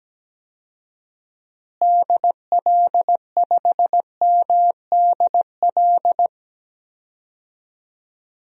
Morse code, heard as a keyed tone, about 17 wpm, 700 Hz. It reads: DL5MDL
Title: DXCC-100 Call Signs - 6 Characters - Repeated 2x - 17wpm